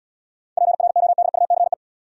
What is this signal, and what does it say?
Morse code, heard as a keyed tone, about 74 wpm, 700 Hz. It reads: PRODUCE